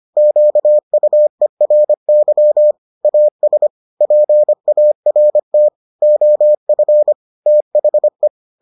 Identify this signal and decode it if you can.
Morse code, heard as a keyed tone, about 25 wpm, 600 Hz. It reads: QUERY AS PART OF THE